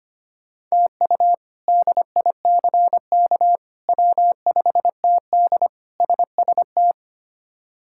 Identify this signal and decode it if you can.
Morse code, heard as a keyed tone, about 25 wpm, 700 Hz. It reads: TU DICK W5TD SST
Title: K1USN SST QSO Practice - 25wpm